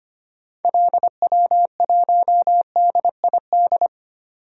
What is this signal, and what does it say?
Morse code, heard as a keyed tone, about 25 wpm, 700 Hz. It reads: LW1DID
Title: DXCC-100 Call Signs - 6 Characters - 25wpm